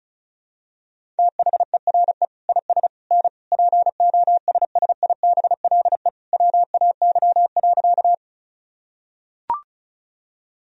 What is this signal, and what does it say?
Morse code, heard as a keyed tone, about 36 wpm, 700 Hz. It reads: THERE IS N POSSIBLE WAY.
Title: Sentences from Top 500 Words and Abbreviations 35wpm